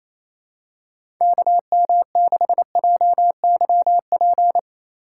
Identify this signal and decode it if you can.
Morse code, heard as a keyed tone, about 28 wpm, 700 Hz. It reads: KM6JYP
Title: USA Call Signs - 8x Character Spacing - 28wpm